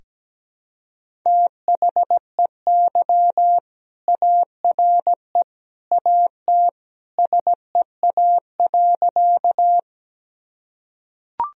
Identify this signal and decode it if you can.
Morse code, heard as a keyed tone, about 17 wpm, 700 Hz. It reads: THEY ARE AT SEA.
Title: Sentences from Top 1000 Words - Review Part 1 - 17wpm